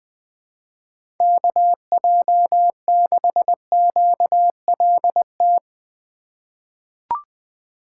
Morse code, heard as a keyed tone, about 21 wpm, 700 Hz. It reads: KJ6QLT
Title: USA Call Signs - 6 Characters - 4x Character Spacing - 20wpm